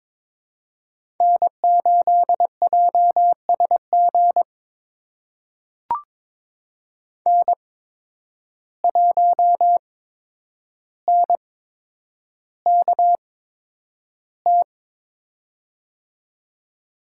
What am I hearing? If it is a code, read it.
Morse code, heard as a keyed tone, about 22 wpm, spaced as for 10 wpm, 700 Hz. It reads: N8JSG N1NKT
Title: USA Call Signs - 5 Characters - 8x Character Spacing - 22wpm